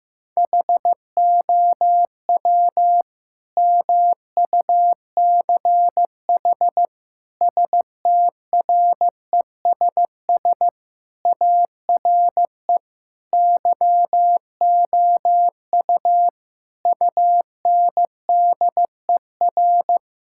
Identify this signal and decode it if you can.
Morse code, heard as a keyed tone, about 15 wpm, 700 Hz. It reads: HOW MUCH STRESS ARE YOU UNDER
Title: Sentences from Top 1000 Words - Review Part 6 - 15wpm